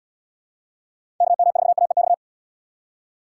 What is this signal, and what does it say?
Morse code, heard as a keyed tone, about 79 wpm, 700 Hz. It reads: KM4AEX